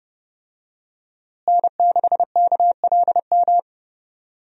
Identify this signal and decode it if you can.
Morse code, heard as a keyed tone, about 32 wpm, 700 Hz. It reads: N6KLM
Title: USA Call Signs - 5 Characters - Repeated 2x - 30wpm